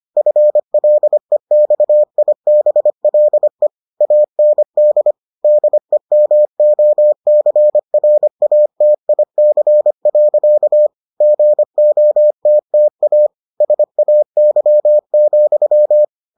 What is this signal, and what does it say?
Morse code, heard as a keyed tone, about 25 wpm, 600 Hz. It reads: FLEXIBLE AND DEMOCRATIC. GOTTA SAY,